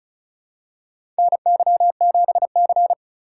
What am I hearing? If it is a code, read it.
Morse code, heard as a keyed tone, about 36 wpm, 700 Hz. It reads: NY7C
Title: USA Call Signs - 4 Characters - 3x Character Spacing - 35wpm